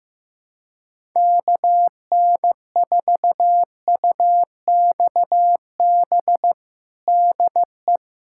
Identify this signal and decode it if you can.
Morse code, heard as a keyed tone, about 15 wpm, 700 Hz. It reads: KN4UXB DE
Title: USA Callsign DE Callsign - 6 Characters - 15wpm